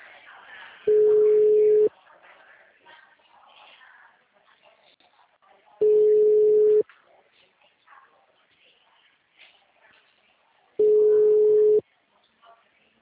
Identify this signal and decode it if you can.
Ringback tone (Europe, 425 Hz), with speech.
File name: out- -2012- - - .1992